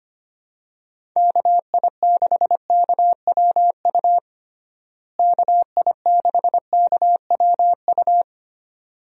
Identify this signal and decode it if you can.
Morse code, heard as a keyed tone, about 25 wpm, 700 Hz. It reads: KI6KWU KI6KWU
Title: USA Call Signs - 6 Characters - Repeated 2x - 25wpm